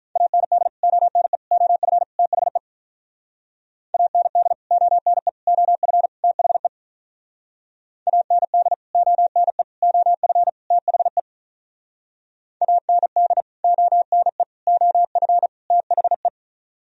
Morse code, heard as a keyed tone, about 44 wpm, 700 Hz. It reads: AND ONE OF THE AND ONE OF THE AND ONE OF THE AND ONE OF THE